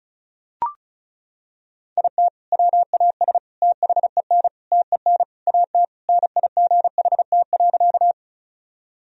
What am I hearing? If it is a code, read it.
Morse code, heard as a keyed tone, about 35 wpm, 700 Hz. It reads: IT WAS THEN TEN AT NIGHT.
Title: Sentences from Top 1000 Words - Review Part 2 - 35wpm